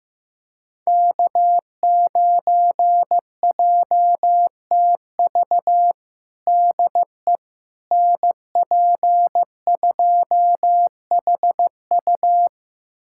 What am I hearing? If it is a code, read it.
Morse code, heard as a keyed tone, about 16 wpm, 700 Hz. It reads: K9JTV DE NP2HU
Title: USA Callsign DE Callsign - 5 Characters - 15wpm